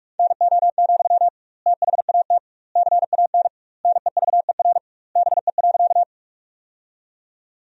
Morse code, heard as a keyed tone, about 46 wpm, 700 Hz. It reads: NO, THAT CAN NEVER BE.
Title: 2x Word-Spacing - Sentences from Top 300 Words - 45wpm